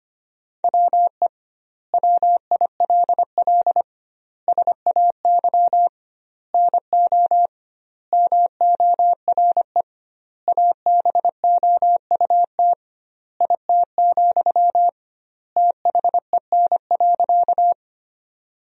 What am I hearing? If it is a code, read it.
Morse code, heard as a keyed tone, about 25 wpm, 700 Hz. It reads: WE WILL SAY NO MORE ABOUT IT, THEN.
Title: 2x Word-Spacing - Sentences from Top 200 Words - 25wpm